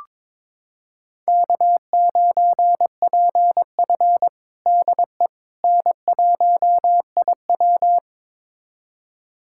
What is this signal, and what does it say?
Morse code, heard as a keyed tone, about 22 wpm, 700 Hz. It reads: K9PF DE N1IW
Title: USA Callsign DE Callsign - 4 Characters - 22wpm